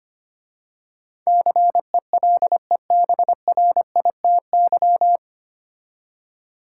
Morse code, heard as a keyed tone, about 25 wpm, 700 Hz. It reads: CELEBRITY
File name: practice / 25wpm / 9 Letter Words - First 3 Letters Given - 25wpm